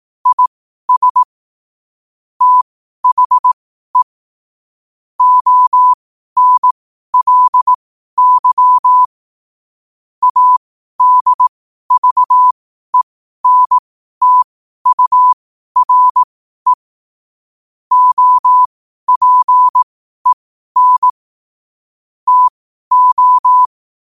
Morse code, heard as a keyed tone, about 18 wpm, spaced as for 12 wpm, 1000 Hz. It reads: IS THE ONLY ADVENTURE OPEN TO
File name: qotdfiles / 12wpm